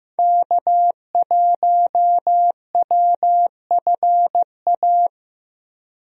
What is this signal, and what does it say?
Morse code, heard as a keyed tone, about 15 wpm, 700 Hz. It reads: K1WFA
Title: USA Call Signs - 5 Characters - Repeated 2x - 15wpm